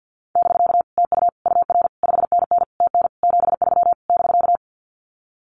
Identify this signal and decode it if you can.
Morse code, heard as a keyed tone, about 50 wpm, 700 Hz. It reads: <BK> TU UR 5NN TN 73 <BK>
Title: Common QSO Phrases Encore 50wpm